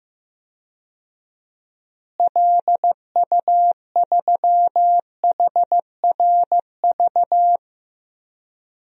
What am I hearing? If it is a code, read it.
Morse code, heard as a keyed tone, about 16 wpm, 700 Hz. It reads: LU3HRV
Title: DXCC-100 Call Signs - 6 Characters - Repeated 3x - 15wpm